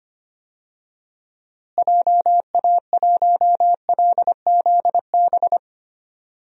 Morse code, heard as a keyed tone, about 25 wpm, 700 Hz. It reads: JA1LZB
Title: DXCC-100 Call Signs - 6 Characters - Repeated 2x - 25wpm